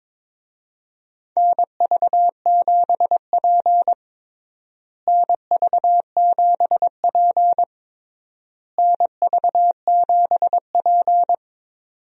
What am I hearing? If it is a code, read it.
Morse code, heard as a keyed tone, about 22 wpm, 700 Hz. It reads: NV7P NV7P NV7P
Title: USA Call Signs - 4 Characters - Repeated 3x - 22wpm